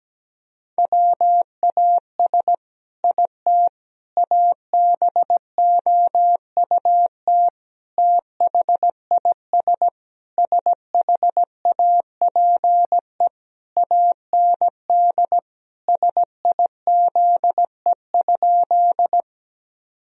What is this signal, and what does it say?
Morse code, heard as a keyed tone, about 17 wpm, 700 Hz. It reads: WAS IT ABOUT THIS SHAPE AND SIZE?
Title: Sentences from Top 1000 Words - Review Part 3 - 17wpm